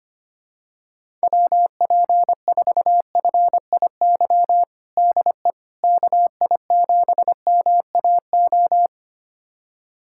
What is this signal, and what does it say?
Morse code, heard as a keyed tone, about 25 wpm, 700 Hz. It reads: WP4FIY DE KI7MAO
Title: USA Callsign DE Callsign 25wpm